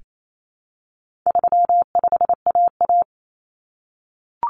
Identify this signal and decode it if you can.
Morse code, heard as a keyed tone, about 28 wpm, 700 Hz. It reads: 35AA
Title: Common Cut Numbers - 4-Digit 28wpm